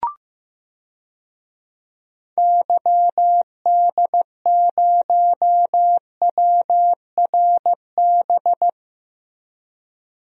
Morse code, heard as a keyed tone, about 15 wpm, 700 Hz. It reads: YD0WRB
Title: DXCC-100 Call Signs - 6 Characters - Repeated 3x - 15wpm